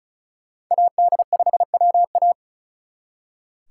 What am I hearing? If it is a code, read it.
Morse code, heard as a keyed tone, about 45 wpm, 700 Hz. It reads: AD5WA